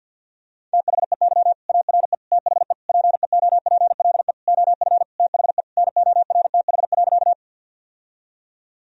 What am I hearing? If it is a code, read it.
Morse code, heard as a keyed tone, about 45 wpm, 700 Hz. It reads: THEY ARE THE PEOPLE OF THE NORTH.
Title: Sentences from Top 1000 Words - Review Part 1 - 50wpm